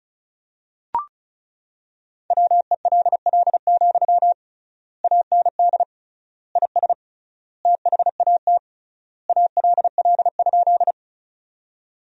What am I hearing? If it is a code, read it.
Morse code, heard as a keyed tone, about 34 wpm, 700 Hz. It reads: WELL, AND IS THAT ALL?
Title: 3x Word-Spacing - Sentences from Top 200 Words - 35wpm